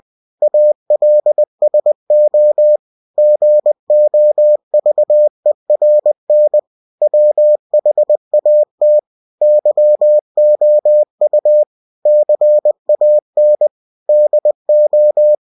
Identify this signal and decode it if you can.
Morse code, heard as a keyed tone, about 20 wpm, 600 Hz. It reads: ALSO GOVERN WHAT YOU CAN DO
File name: cw-books / winnie-the-pooh / winnie-20wpm-page-0114